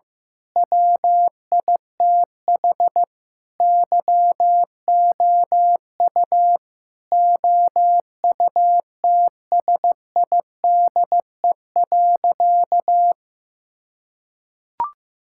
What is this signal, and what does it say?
Morse code, heard as a keyed tone, about 15 wpm, 700 Hz. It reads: WITH YOU OUTSIDE.
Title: Sentences from Top 1000 Words - Review Part 6 - 15wpm